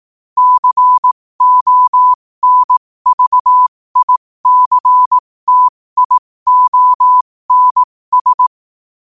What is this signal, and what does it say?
Morse code, heard as a keyed tone, about 18 wpm, spaced as for 15 wpm, 1000 Hz. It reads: CONVICTIONS
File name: qotdfiles / 15wpm